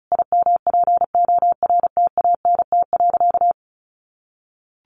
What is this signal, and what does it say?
Morse code, heard as a keyed tone, about 34 wpm, 700 Hz. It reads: IMPORTANT.
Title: Sentences from Top 1200 Words 35wpm